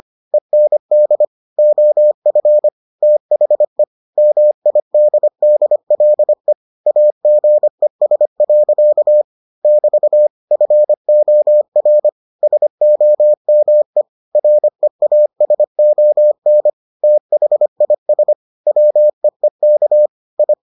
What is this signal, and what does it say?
Morse code, heard as a keyed tone, about 25 wpm, 600 Hz. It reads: END OF THE MIDDLE AGES. <BT> FOR SOME REASON THIS WEEK I